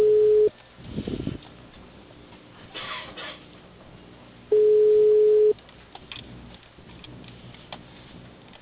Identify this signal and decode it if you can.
Ringback tone (Europe, 425 Hz).
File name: out- -2046- - - .26769